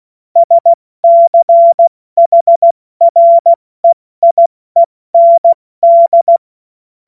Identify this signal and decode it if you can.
Morse code, heard as a keyed tone, about 16 wpm, 680 Hz. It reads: SCHREIEND
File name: Morsekurs / Witze / Witz 25. April 70 F80